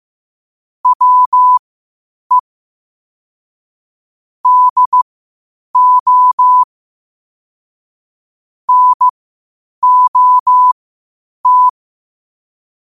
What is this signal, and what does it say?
Morse code, heard as a keyed tone, about 15 wpm, spaced as for 8 wpm, 1000 Hz. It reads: WE DO NOT
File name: qotdfiles / 8wpm